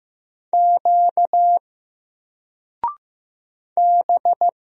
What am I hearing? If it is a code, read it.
Morse code, heard as a keyed tone, about 15 wpm, 700 Hz. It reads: Q B
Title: Single Character (Letter:Number:Punctuation:Prosign) 15wpm